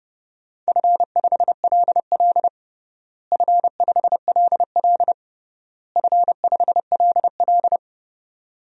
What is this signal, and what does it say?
Morse code, heard as a keyed tone, about 30 wpm, 700 Hz. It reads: F5LL F5LL F5LL
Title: DXCC-100 Call Signs - 4 Characters - Repeated 3x - 30wpm